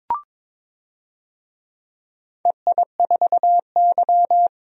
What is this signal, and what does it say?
Morse code, heard as a keyed tone, about 22 wpm, 700 Hz. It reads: EI4Y
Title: DXCC-100 Call Signs - 4 Characters - Repeated 2x - 22wpm